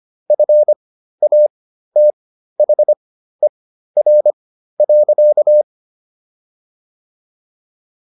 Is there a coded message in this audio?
Yes: Morse code from 0.3 to 5.6 s, heard as a keyed tone, about 25 wpm, spaced as for 13 wpm, 600 Hz. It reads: FATHER.